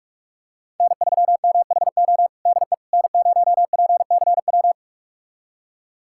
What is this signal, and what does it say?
Morse code, heard as a keyed tone, about 46 wpm, 700 Hz. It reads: N3MHO DE N0PCW
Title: USA Callsign DE Callsign - 5 Characters - 45wpm